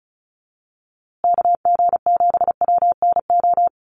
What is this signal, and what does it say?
Morse code, heard as a keyed tone, about 34 wpm, 700 Hz. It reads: KG7WNO
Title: USA Call Signs Repeated 2x 35wpm